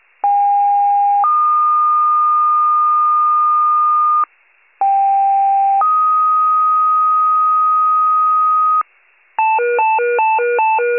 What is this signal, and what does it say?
Two-tone page, heard as a one-second tone, then a three-second tone, about 810 then 1220 Hz, about 790 then 1250 Hz. It is heard from 0.2 to 8.8 s.
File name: wcalert